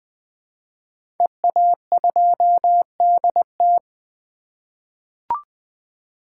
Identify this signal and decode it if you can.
Morse code, heard as a keyed tone, about 20 wpm, 700 Hz. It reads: EA2DT
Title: 2023 SOTA Chaser Call Signs - 20wpm